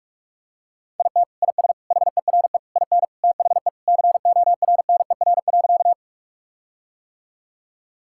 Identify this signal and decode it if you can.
Morse code, heard as a keyed tone, about 41 wpm, 700 Hz. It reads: IT IS HERE IN THE CORNER.
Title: Sentences from Top 1600 Words 45wpm